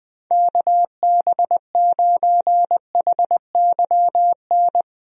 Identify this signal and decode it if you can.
Morse code, heard as a keyed tone, about 20 wpm, 700 Hz. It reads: KB9HYN
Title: USA Call Signs - 6 Characters - 20wpm